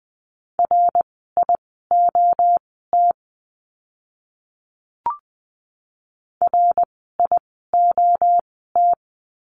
Morse code, heard as a keyed tone, about 20 wpm, spaced as for 14 wpm, 700 Hz. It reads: RIOT RIOT
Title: 061 Lesson 11f - Letter h - Words 20wpm